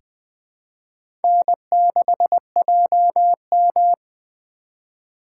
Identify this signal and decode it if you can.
Morse code, heard as a keyed tone, about 20 wpm, 700 Hz. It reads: N6JM
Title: USA Call Signs - 4 Characters - 2x Character Spacing - 20wpm